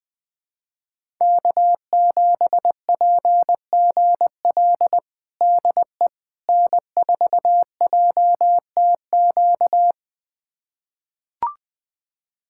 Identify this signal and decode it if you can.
Morse code, heard as a keyed tone, about 20 wpm, 700 Hz. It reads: K7PGL DE N4JTQ
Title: USA Callsign DE Callsign - 5 Characters - 20wpm